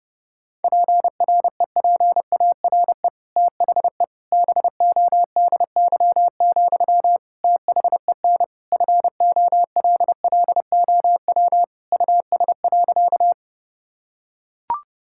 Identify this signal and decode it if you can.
Morse code, heard as a keyed tone, about 30 wpm, 700 Hz. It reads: PREPARE THE BODY, THEN FOLLOW US.
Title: Sentences from Top 1000 Words - Review Part 6 - 30wpm